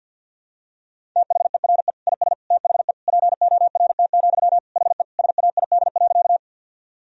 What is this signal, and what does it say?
Morse code, heard as a keyed tone, about 47 wpm, 700 Hz. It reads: THERE IS THE PORT, HE SAID.
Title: Sentences from Top 1000 Words - Review Part 1 - 50wpm